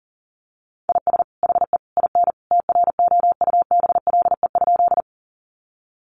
Morse code, heard as a keyed tone, about 40 wpm, 700 Hz. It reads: IS HE IN TROUBLE?
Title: Sentences from Top 1200 Words 40wpm